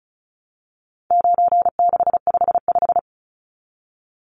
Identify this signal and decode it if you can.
Morse code, heard as a keyed tone, about 34 wpm, 700 Hz. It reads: 9655
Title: Numbers - 4-Digit 35wpm